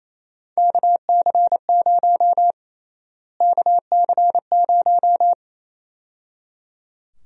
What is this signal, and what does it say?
Morse code, heard as a keyed tone, about 28 wpm, 700 Hz. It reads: KC0 KC0
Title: USA Call Sign Prefixes - Repeated 2x - 28wpm